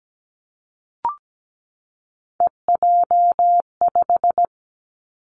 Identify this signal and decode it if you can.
Morse code, heard as a keyed tone, about 17 wpm, 700 Hz. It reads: EJ5
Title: Three Challenging Letter-Number Combinations 17wpm